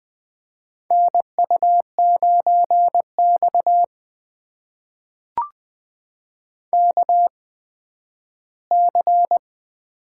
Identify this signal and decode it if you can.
Morse code, heard as a keyed tone, about 20 wpm, spaced as for 12 wpm, 700 Hz. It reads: NU9X KC
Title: USA Call Signs - 8x Character Spacing - 20wpm